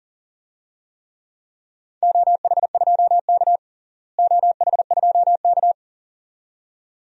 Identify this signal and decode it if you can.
Morse code, heard as a keyed tone, about 37 wpm, 700 Hz. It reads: OH2K OH2K
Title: DXCC-100 Call Signs - 4 Characters - Repeated 2x - 40wpm